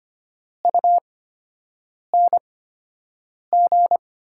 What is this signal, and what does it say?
Morse code, heard as a keyed tone, about 25 wpm, spaced as for 7 wpm, 700 Hz. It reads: UNG
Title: 8x Character Spacing - Top 500 Words - 25wpm